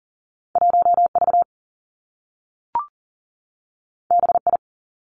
Morse code, heard as a keyed tone, about 38 wpm, 700 Hz. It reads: 1 V B I
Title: Two Challenging Letter-Number Combinations 40wpm